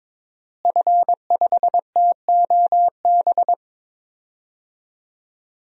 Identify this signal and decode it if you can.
Morse code, heard as a keyed tone, about 22 wpm, 700 Hz. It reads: F5TOB
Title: DXCC-100 Call Signs - 5 Characters - Repeated 3x - 22wpm